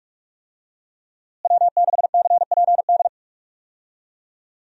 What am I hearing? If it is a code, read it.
Morse code, heard as a keyed tone, about 44 wpm, 700 Hz. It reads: W6CPD
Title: USA Call Signs - 5 Characters - Repeated 3x - 45wpm